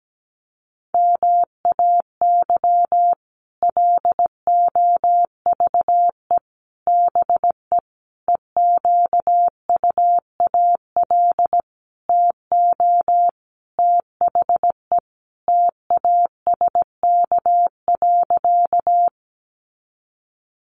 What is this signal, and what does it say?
Morse code, heard as a keyed tone, about 17 wpm, 700 Hz. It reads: MAY LOVE BE EQUAL TO THE TASK.
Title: Sentences from Top 1400 Words 17wpm